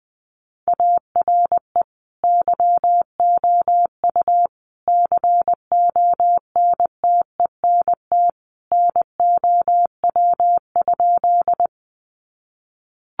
Morse code, heard as a keyed tone, about 20 wpm, 700 Hz. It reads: ARE YOU CONTENT NOW?
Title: Sentences from Top 900 Words 20wpm